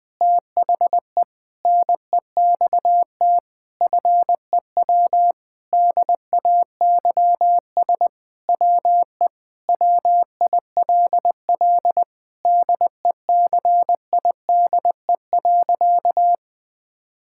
Morse code, heard as a keyed tone, about 20 wpm, 700 Hz. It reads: THE NEXT FEW DAYS WE WILL DECIDE.
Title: Sentences with 7-Word Phrases - 20wpm